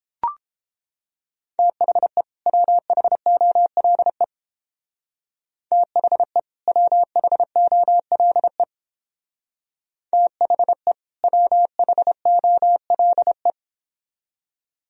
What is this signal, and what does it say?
Morse code, heard as a keyed tone, about 29 wpm, 700 Hz. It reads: THE WHOLE THE WHOLE THE WHOLE